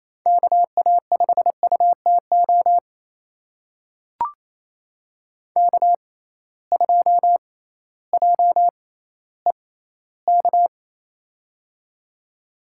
Morse code, heard as a keyed tone, about 28 wpm, spaced as for 15 wpm, 700 Hz. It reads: KA5UTO K2JEK
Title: USA Call Signs - 6x Character Spacing - 28wpm